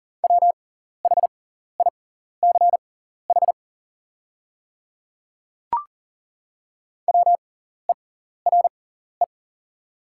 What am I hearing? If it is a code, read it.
Morse code, heard as a keyed tone, about 40 wpm, spaced as for 14 wpm, 700 Hz. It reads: WHICH WERE
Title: 6x Character Spacing - Top 100 Words - 40wpm